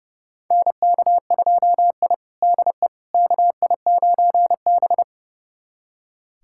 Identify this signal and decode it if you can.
Morse code, heard as a keyed tone, about 32 wpm, 700 Hz. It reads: NK2I DE KI9B
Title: USA Callsign DE Callsign - 4 Characters - 30wpm